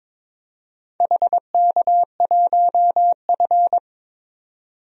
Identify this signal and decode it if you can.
Morse code, heard as a keyed tone, about 22 wpm, 700 Hz. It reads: HK1F
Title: DXCC-100 Call Signs - 4 Characters - 22wpm